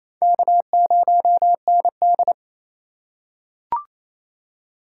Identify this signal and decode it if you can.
Morse code, heard as a keyed tone, about 28 wpm, 700 Hz. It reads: K0ND
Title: USA Call Signs - 4 Characters - 4x Character Spacing - 28wpm